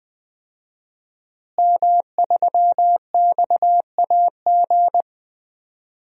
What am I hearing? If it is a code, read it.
Morse code, heard as a keyed tone, about 20 wpm, 700 Hz. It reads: M3XAG
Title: DXCC-100 Call Signs - 5 Characters - Repeated 3x - 20wpm